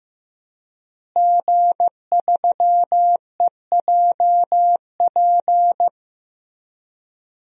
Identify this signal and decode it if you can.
Morse code, heard as a keyed tone, about 15 wpm, 700 Hz. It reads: G3EJP
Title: DXCC-100 Call Signs - 5 Characters - Repeated 3x - 15wpm